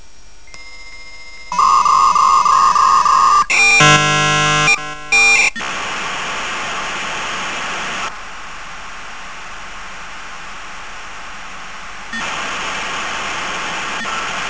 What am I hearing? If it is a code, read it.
Dial-up modem handshake.
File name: V.90 - Momenta 56DSP